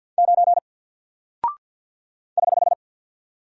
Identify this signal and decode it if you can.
Morse code, heard as a keyed tone, about 48 wpm, 700 Hz. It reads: 9 <HH>
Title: Single Character (Letter:Number:Punctuation:Prosign) 50wpm